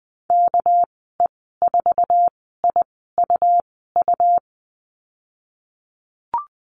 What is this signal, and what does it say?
Morse code, heard as a keyed tone, about 20 wpm, spaced as for 14 wpm, 700 Hz. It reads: KE4IUU
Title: 231 Lesson 37g - Letter x - Call Signs 20wpm